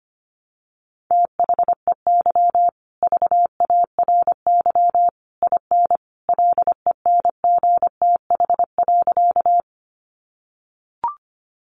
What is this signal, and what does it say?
Morse code, heard as a keyed tone, about 25 wpm, 700 Hz. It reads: THEY VARY IN LENGTH.
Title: Sentences from Top 1200 Words 25wpm